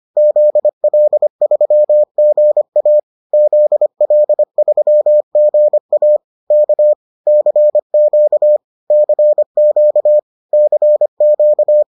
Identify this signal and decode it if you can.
Morse code, heard as a keyed tone, about 25 wpm, 600 Hz. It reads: ZL3GA ZL3GA K CQ CQ CQ